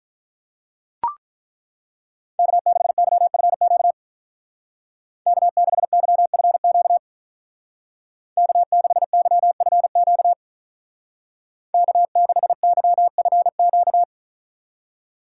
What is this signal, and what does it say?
Morse code, heard as a keyed tone, about 43 wpm, 700 Hz. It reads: K6YFQ K6YFQ K6YFQ K6YFQ
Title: Speed Racing - USA 1x3 Call Signs 35wpm